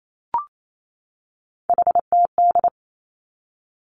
Morse code, heard as a keyed tone, about 28 wpm, 700 Hz. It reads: HTD
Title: Three Challenging Letter-Number Combinations 28wpm